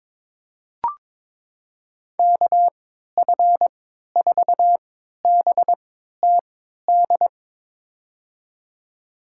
Morse code, heard as a keyed tone, about 21 wpm, spaced as for 12 wpm, 700 Hz. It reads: KF4BTD
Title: USA Call Signs - 6 Characters - 3x Character Spacing - 22wpm